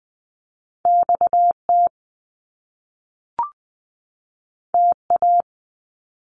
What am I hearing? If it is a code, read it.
Morse code, heard as a keyed tone, about 20 wpm, 700 Hz. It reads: XT TA
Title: Top Two Letter Combinations in Words 20wpm